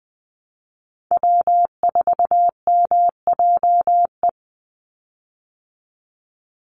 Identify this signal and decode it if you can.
Morse code, heard as a keyed tone, about 20 wpm, 700 Hz. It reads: W4MJE
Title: Call Signs Repeated 2x - 20wpm